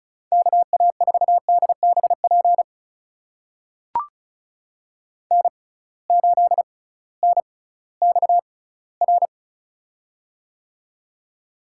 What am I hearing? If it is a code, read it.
Morse code, heard as a keyed tone, about 34 wpm, spaced as for 19 wpm, 700 Hz. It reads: KA4DBP N8NXR
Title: USA Call Signs - 6x Character Spacing - 35wpm